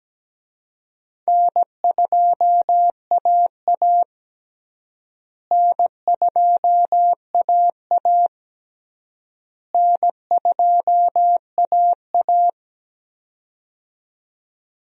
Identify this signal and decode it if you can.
Morse code, heard as a keyed tone, about 17 wpm, 700 Hz. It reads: N2AA N2AA N2AA
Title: USA Call Signs - 4 Characters - Repeated 3x - 17wpm